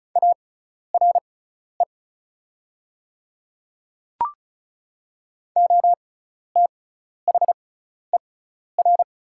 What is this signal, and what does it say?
Morse code, heard as a keyed tone, about 36 wpm, spaced as for 12 wpm, 700 Hz. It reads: ARE OTHER
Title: 6x Character Spacing - Top 100 Words - 35wpm